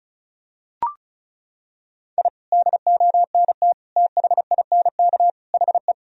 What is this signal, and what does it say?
Morse code, heard as a keyed tone, about 36 wpm, 700 Hz. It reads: I DONT THINK HE
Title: Sentences from Top 1000 Words - Review Part 4 - 35wpm